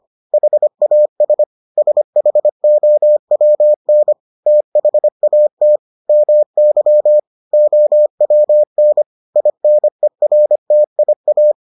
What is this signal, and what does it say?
Morse code, heard as a keyed tone, about 25 wpm, 600 Hz. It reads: HAS SHOWN THAT MY OWN INERTIA